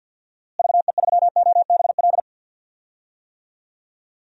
Morse code, heard as a keyed tone, about 52 wpm, 700 Hz. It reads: VE3OBL
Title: DXCC-100 Call Signs - 6 Characters - Repeated 2x - 50wpm